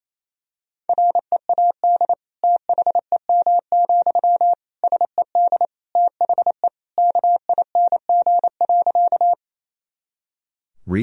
Morse code, heard as a keyed tone, about 28 wpm, 700 Hz. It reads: READ THEM, SED THE KING.
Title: Sentences from Top 400 Words and Abbreviations 28wpm